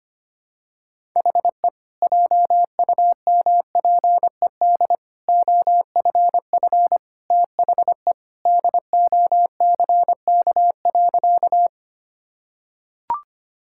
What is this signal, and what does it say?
Morse code, heard as a keyed tone, about 25 wpm, 700 Hz. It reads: HE JUMPED OFF THE DOCK.
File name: practice / 25wpm / Sentences with 2-Word Phrases - 25wpm